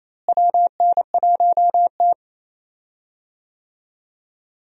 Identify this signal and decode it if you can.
Morse code, heard as a keyed tone, about 27 wpm, 700 Hz. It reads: WN1T
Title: USA Call Signs - 4 Characters - Repeated 3x - 28wpm